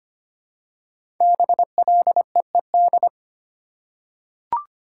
Morse code, heard as a keyed tone, about 25 wpm, 700 Hz. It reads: BLEED
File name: 5 Letter Words - First 3 Letters Given - 25wpm